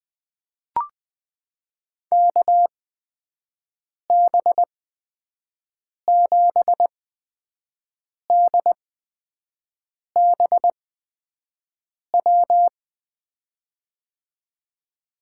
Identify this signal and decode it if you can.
Morse code, heard as a keyed tone, about 20 wpm, spaced as for 5 wpm, 700 Hz. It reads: KB7DBW
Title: USA Call Signs - 6 Characters - 8x Character Spacing - 20wpm